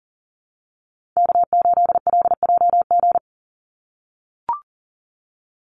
Morse code, heard as a keyed tone, about 40 wpm, 700 Hz. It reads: K8LJG
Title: NA SOTA Chaser Callsigns 40wpm